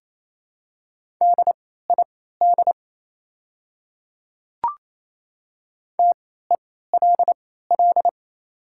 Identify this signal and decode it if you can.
Morse code, heard as a keyed tone, about 27 wpm, spaced as for 16 wpm, 700 Hz. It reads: DID TELL
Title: 3x Character Spacing - Top 200 Words - 28wpm